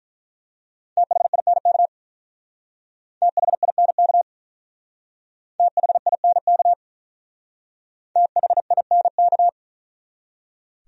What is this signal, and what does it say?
Morse code, heard as a keyed tone, about 43 wpm, 700 Hz. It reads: THINK THINK THINK THINK
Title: Speed-Racing Top 300 Words 35wpm